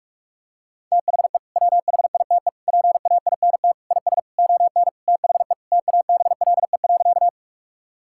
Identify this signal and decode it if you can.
Morse code, heard as a keyed tone, about 46 wpm, 700 Hz. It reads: THE WHITE PAINT IS ON THE TABLE.
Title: Sentences from Top 1000 Words - Review Part 6 - 45wpm